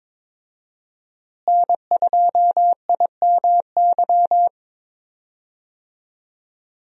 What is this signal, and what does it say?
Morse code, heard as a keyed tone, about 22 wpm, 700 Hz. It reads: N2IMY
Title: USA Call Signs - 5 Characters - Repeated 2x - 22wpm